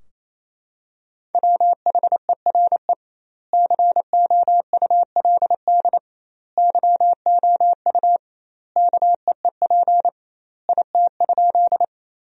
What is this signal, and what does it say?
Morse code, heard as a keyed tone, about 27 wpm, 700 Hz. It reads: WHERE COULD YOU KEEP IT?
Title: 2x Word-Spacing - Sentences from Top 300 Words - 28wpm